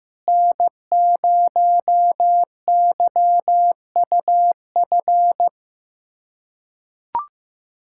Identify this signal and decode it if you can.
Morse code, heard as a keyed tone, about 16 wpm, 700 Hz. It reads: N0YUF
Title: USA Callsign DE Callsign - 5 Characters - 15wpm